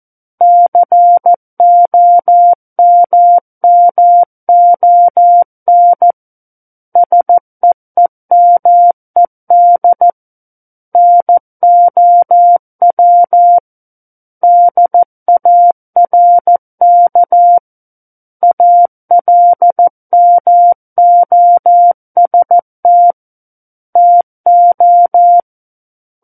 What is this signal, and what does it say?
Morse code, heard as a keyed tone, about 14 wpm, 700 Hz. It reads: COMMON SEEMED NOW DARK ALMOST TO
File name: Book One / 05-Chapter 5 - 14 WPM